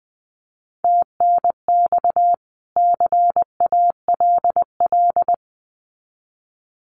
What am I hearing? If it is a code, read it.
Morse code, heard as a keyed tone, about 20 wpm, 700 Hz. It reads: TNX CALL
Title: Common QSO Phrases 20wpm